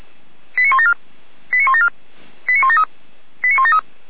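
Five-tone selective call (ZVEI), heard as quick runs of five tones, tones of about 70 ms. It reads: 89174 89174 89173 89173
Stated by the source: ZVEI five-tone selective call, code 89250, received by SDR; recording hosted by feuerwehr-orlamuende.org (fire brigade Orlamünde).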